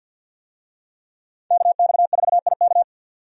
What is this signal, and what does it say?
Morse code, heard as a keyed tone, about 47 wpm, 700 Hz. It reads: KX4IK
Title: USA Call Signs - 5 Characters - 50wpm